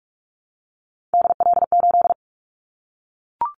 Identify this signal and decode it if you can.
Morse code, heard as a keyed tone, about 44 wpm, 700 Hz. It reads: DL8
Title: Callsign Prefixes for Top 25 DXCC - Repeated 3x - 45wpm